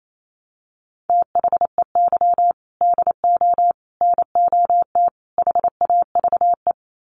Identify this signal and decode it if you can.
Morse code, heard as a keyed tone, about 28 wpm, 700 Hz. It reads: THEY DO NOT HAVE
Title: Sets of 4 Words 28wpm